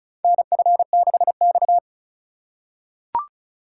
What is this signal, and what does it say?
Morse code, heard as a keyed tone, about 36 wpm, 700 Hz. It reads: NF6X
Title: USA Call Signs - 4x Character Spacing - 35wpm